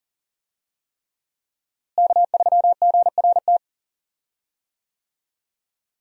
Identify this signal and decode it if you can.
Morse code, heard as a keyed tone, about 40 wpm, 700 Hz. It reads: K3GRT
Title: USA Call Signs - 5 Characters - Repeated 3x - 40wpm